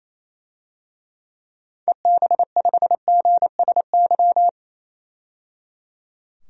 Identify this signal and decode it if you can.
Morse code, heard as a keyed tone, about 28 wpm, 700 Hz. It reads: EB5GSY
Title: DXCC-100 Call Signs - 6 Characters - 28wpm